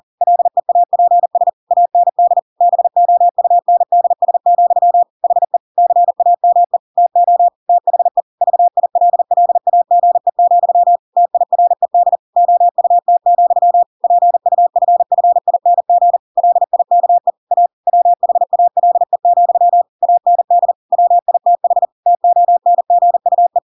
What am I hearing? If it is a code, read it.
Morse code, heard as a keyed tone, about 40 wpm, 700 Hz. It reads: LEAPS AND BOUNDS, HE CAME TO THE VILLAGE, TIRED OUT, PUFFING LIKE A WHALE, AND WITH TONGUE